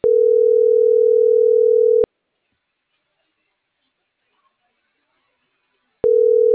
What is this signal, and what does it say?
Ringback tone (North America), 440 + 480 Hz.